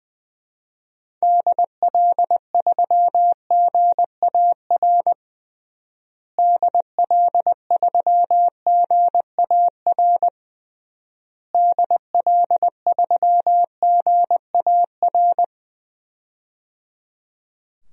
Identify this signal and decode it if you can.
Morse code, heard as a keyed tone, about 20 wpm, 700 Hz. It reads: DL3GAR DL3GAR DL3GAR
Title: DXCC-100 Call Signs - 6 Characters - Repeated 3x - 20wpm